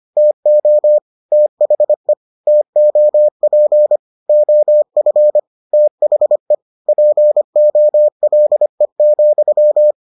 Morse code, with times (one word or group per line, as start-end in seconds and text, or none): 0.2-1.0 TO
1.3-2.1 THE
2.5-4.0 TOP
4.3-5.4 OF
5.7-6.5 THE
6.9-9.9 POLE,